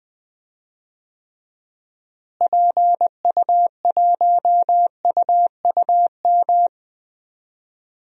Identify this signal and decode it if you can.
Morse code, heard as a keyed tone, about 21 wpm, 700 Hz. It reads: PU1UUM
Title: DXCC-100 Call Signs - 6 Characters - Repeated 2x - 20wpm